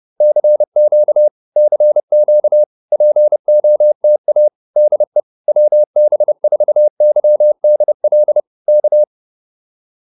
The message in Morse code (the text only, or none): CQ CQ POTA DE WB4YDL K